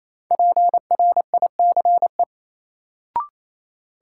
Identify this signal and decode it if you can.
Morse code, heard as a keyed tone, about 28 wpm, 700 Hz. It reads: PRICE
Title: Sentences from Top 600 Words - New Word Drill - 28wpm